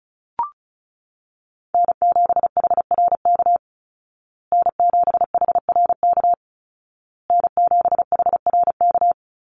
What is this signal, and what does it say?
Morse code, heard as a keyed tone, about 34 wpm, 700 Hz. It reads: N7HRK N7HRK N7HRK
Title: Call Signs Repeated 2x - 35wpm